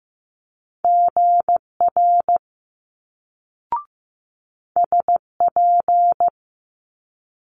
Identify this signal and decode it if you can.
Morse code, heard as a keyed tone, about 15 wpm, 700 Hz. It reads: GR SP
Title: Multi-letter Phonemes Quick Fire 15wpm